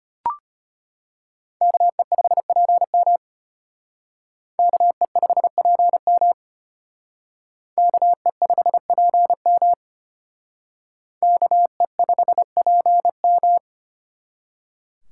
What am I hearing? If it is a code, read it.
Morse code, heard as a keyed tone, about 32 wpm, 700 Hz. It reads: KE5PM KE5PM KE5PM KE5PM